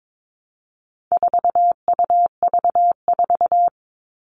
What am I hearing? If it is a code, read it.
Morse code, heard as a keyed tone, about 22 wpm, 700 Hz. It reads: 4UV4
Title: All Cut Numbers - 4-Digit 22wpm